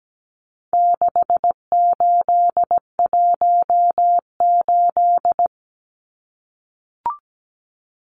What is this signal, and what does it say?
Morse code, heard as a keyed tone, about 17 wpm, 700 Hz. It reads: 6818
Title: Numbers - 4-Digit 17wpm